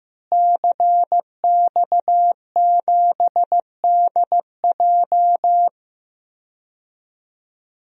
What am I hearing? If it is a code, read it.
Morse code, heard as a keyed tone, about 15 wpm, 700 Hz. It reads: CX7DJ
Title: DXCC-100 Call Signs - 5 Characters - Repeated 3x - 15wpm